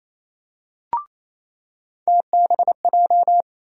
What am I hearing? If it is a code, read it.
Morse code, heard as a keyed tone, about 28 wpm, 700 Hz. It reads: TBJ
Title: USA Call Sign Postfixes - 3 Letters 28wpm